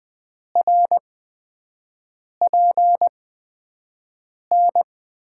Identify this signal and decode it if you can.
Morse code, heard as a keyed tone, about 20 wpm, spaced as for 5 wpm, 700 Hz. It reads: RPN
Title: USA Call Signs - 6 Characters - 8x Character Spacing - 20wpm